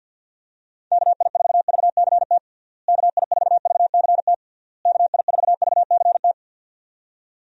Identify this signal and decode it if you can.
Morse code, heard as a keyed tone, about 46 wpm, 700 Hz. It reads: KI4VCT KI4VCT KI4VCT
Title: USA Call Signs - 6 Characters - Repeated 3x - 50wpm